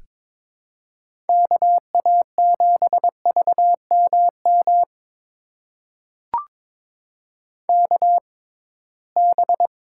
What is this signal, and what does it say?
Morse code, heard as a keyed tone, about 22 wpm, spaced as for 17 wpm, 700 Hz. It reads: KA7VMM KB
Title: USA Call Signs - 6 Characters - 6x Character Spacing - 22wpm